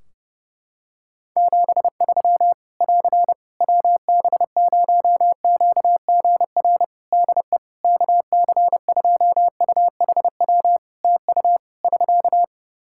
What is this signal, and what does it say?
Morse code, heard as a keyed tone, about 32 wpm, 700 Hz. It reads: 73 <AR> WB0QGR DE KC2UHW TU <SK>
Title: Practice QSO - Exchange 4 - Ending - 30wpm